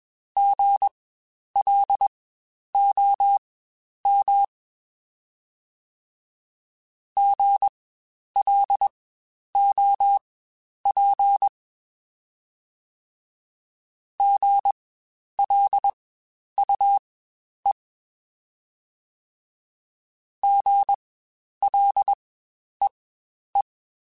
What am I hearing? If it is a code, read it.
Morse code, heard as a keyed tone, about 21 wpm, spaced as for 10 wpm, 800 Hz. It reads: GLOM GLOP GLUE GLEE
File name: Spaced Alphas